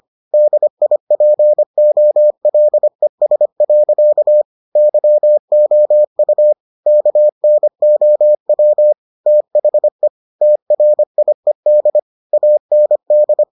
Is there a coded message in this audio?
Yes: Morse code from 0.3 to 13.4 s, heard as a keyed tone, about 25 wpm, 600 Hz. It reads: DIPOLES. YOU KNOW THE TRIED AND